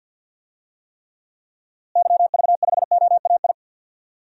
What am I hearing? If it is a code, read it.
Morse code, heard as a keyed tone, about 47 wpm, 700 Hz. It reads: YV5OAI
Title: DXCC-100 Call Signs - 6 Characters - Repeated 3x - 50wpm